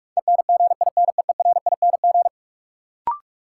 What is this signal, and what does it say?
Morse code, heard as a keyed tone, about 46 wpm, 700 Hz. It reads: ENGINEERING
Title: Sentences from Top 1500 Words - New Word Drill - 45wpm